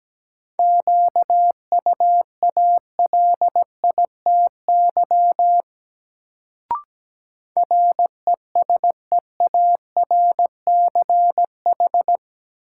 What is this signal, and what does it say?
Morse code, heard as a keyed tone, about 17 wpm, 700 Hz. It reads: QUALITY RESEARCH
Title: Sentences from Top 1000 Words - Review Part 4 - New Word Drill - 17wpm